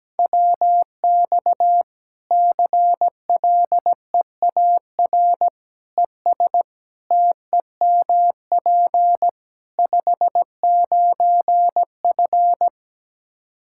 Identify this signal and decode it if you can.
Morse code, heard as a keyed tone, about 17 wpm, 700 Hz. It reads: WX CLEAR ES TEMP 59F
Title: K1USN SST QSO Practice 17wpm